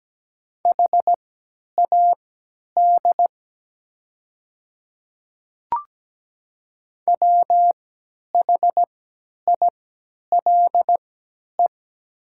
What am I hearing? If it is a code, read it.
Morse code, heard as a keyed tone, about 17 wpm, spaced as for 10 wpm, 690 Hz. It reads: HAD WHILE
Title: 3x Character Spacing - Top 300 Words - 17wpm